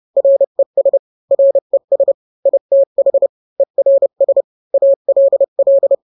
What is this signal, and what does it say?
Morse code, heard as a keyed tone, about 30 wpm, 550 Hz. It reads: RES RES ITH ERS ALL